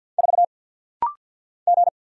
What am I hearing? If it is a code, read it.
Morse code, heard as a keyed tone, about 48 wpm, 700 Hz. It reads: 4 G
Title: Single Letter-Number - Rapid-Fire 50wpm